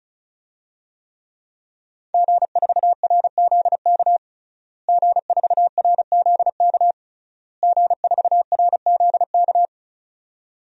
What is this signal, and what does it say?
Morse code, heard as a keyed tone, about 34 wpm, 700 Hz. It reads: G4RZK G4RZK G4RZK
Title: DXCC-100 Call Signs - 5 Characters - Repeated 3x - 35wpm